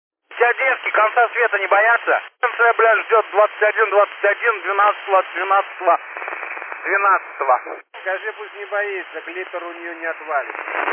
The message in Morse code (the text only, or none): none